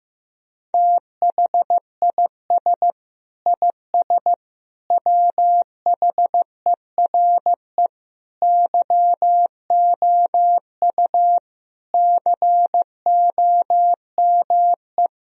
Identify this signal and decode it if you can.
Morse code, heard as a keyed tone, about 15 wpm, 700 Hz. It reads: THIS IS WHERE YOU COME
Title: Sentences from Top 1000 Words - Review Part 3 - 15wpm